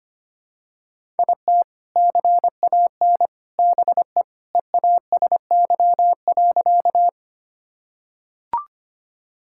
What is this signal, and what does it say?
Morse code, heard as a keyed tone, about 25 wpm, 700 Hz. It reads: IT CAN BE EASY.
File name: practice / 25wpm / Sentences with 3-Word Phrases - 25wpm